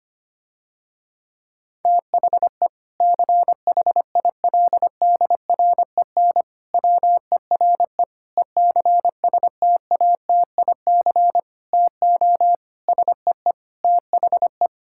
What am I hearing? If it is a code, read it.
Morse code, heard as a keyed tone, about 25 wpm, 700 Hz. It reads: THE CHILDREN WERE ECSTATIC TO SEE THE
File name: practice / 25wpm / Difficult Sentences with Hints - 25wpm